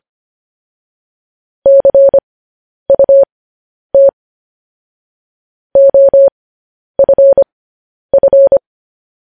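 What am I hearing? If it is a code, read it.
Morse code, heard as a keyed tone, about 25 wpm, spaced as for 10 wpm, 550 Hz. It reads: CUT OFF
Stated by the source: Morse code news headlines sent at 10 wpm for practice.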